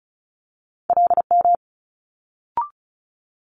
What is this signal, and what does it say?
Morse code, heard as a keyed tone, about 36 wpm, 700 Hz. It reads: LM
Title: Multi-letter Phonemes Quick Fire 35wpm